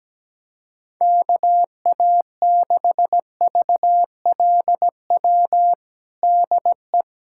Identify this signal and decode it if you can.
Morse code, heard as a keyed tone, about 17 wpm, 700 Hz. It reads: KA6VLW DE
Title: USA Callsign DE Callsign - 6 Characters - 17wpm